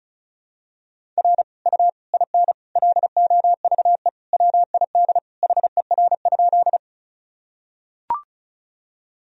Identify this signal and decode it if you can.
Morse code, heard as a keyed tone, about 36 wpm, 700 Hz. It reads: R U IN LOVE WID HER?
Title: Sentences from Top 400 Words and Abbreviations 35wpm